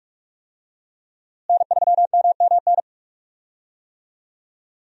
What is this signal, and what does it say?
Morse code, heard as a keyed tone, about 44 wpm, 700 Hz. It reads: N3MMN
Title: USA Call Signs - 5 Characters - Repeated 3x - 45wpm